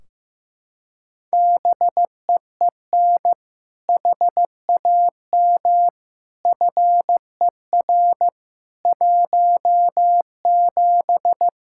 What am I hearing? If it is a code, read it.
Morse code, heard as a keyed tone, about 14 wpm, 700 Hz. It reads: BEEN HAM FER 17
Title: Practice QSO - Exchange 3 - More Info - 15wpm